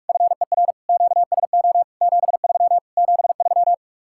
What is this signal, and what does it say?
Morse code, heard as a keyed tone, about 44 wpm, 700 Hz. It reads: FER QSO 73 73